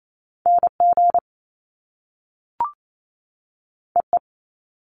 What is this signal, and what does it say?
Morse code, heard as a keyed tone, about 28 wpm, 700 Hz. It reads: NG EE
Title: Top Two Letter Combinations in Words 28wpm